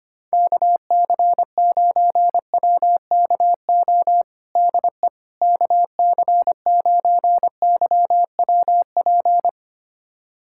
Morse code, heard as a keyed tone, about 25 wpm, 700 Hz. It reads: KC9WKO DE KC9YWP
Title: USA Callsign DE Callsign 25wpm